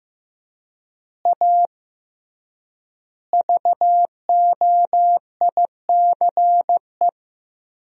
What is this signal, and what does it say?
Morse code, heard as a keyed tone, about 16 wpm, 700 Hz. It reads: A VOICE
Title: 3x Word-Spacing - Sentences from Top 500 Words - 15wpm